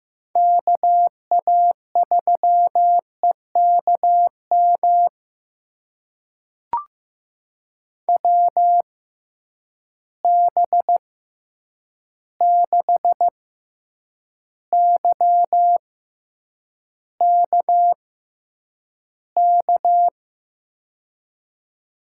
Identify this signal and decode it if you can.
Morse code, heard as a keyed tone, about 16 wpm, spaced as for 8 wpm, 700 Hz. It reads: KA3EKM WB6YKK
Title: USA Call Signs - 6 Characters - 6x Character Spacing - 15wpm